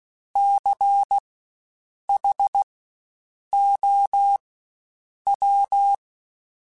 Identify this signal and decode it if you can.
Morse code, heard as a keyed tone, about 16 wpm, spaced as for 7 wpm, 800 Hz. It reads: CHOW
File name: Spaced Alphas / Spaced Alphas mp3s / Cs sp @15